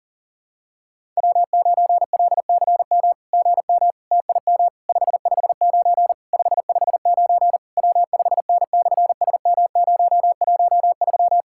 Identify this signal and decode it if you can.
Morse code, heard as a keyed tone, about 37 wpm, 700 Hz. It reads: W9LCM GM TIM 559 559 W5N/SM013
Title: SOTA QSO Practice 40wpm